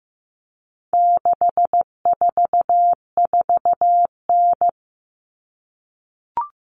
Morse code, heard as a keyed tone, about 15 wpm, 700 Hz. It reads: 644N
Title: All Cut Numbers - 4-Digit 15wpm